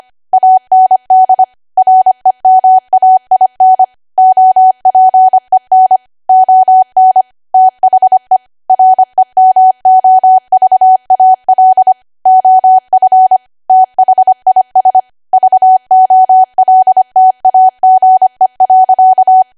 Morse code, heard as a keyed tone, about 25 wpm, 750 Hz. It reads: AND REMAIN OPEN ON THE REMOVAL OF THIS VOLTAGE.